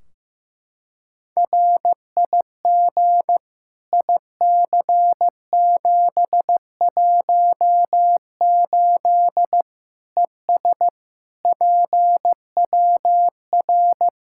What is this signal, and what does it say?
Morse code, heard as a keyed tone, about 15 wpm, 700 Hz. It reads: RIG IC718 ES PWR
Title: K1USN SST QSO Practice 15wpm